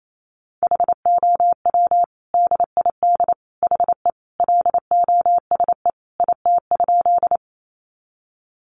Morse code, heard as a keyed tone, about 28 wpm, 700 Hz. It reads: HOW DID HE LOSE IT?
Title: Sentences from Top 1100 Words 28wpm